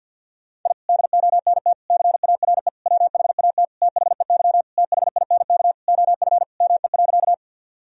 Morse code, heard as a keyed tone, about 52 wpm, 700 Hz. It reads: I DONT CARE WHAT THEY THINK OF ME.
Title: Sentences from Top 1000 Words - Review Part 1 - 50wpm